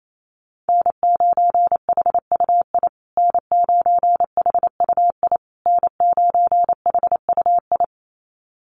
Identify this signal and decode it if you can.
Morse code, heard as a keyed tone, about 28 wpm, 700 Hz. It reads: N9HUI N9HUI N9HUI
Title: USA Call Signs Repeated 3x 28wpm